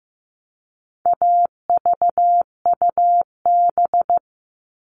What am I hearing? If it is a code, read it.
Morse code, heard as a keyed tone, about 15 wpm, 700 Hz. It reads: AVUB
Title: All Cut Numbers - 4-Digit 15wpm